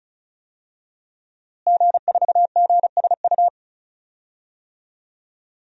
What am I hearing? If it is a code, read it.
Morse code, heard as a keyed tone, about 36 wpm, 700 Hz. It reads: G4GSU
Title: DXCC-100 Call Signs - 5 Characters - Repeated 3x - 35wpm